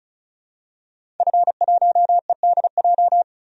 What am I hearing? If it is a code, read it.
Morse code, heard as a keyed tone, about 34 wpm, 700 Hz. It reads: F1EDJ
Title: DXCC-100 Call Signs - 5 Characters - 35wpm